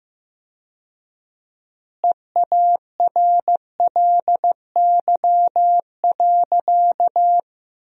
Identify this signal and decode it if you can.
Morse code, heard as a keyed tone, about 16 wpm, 700 Hz. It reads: EARLY.
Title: 4x Word-Spacing - Sentences from Top 400 Words - 15wpm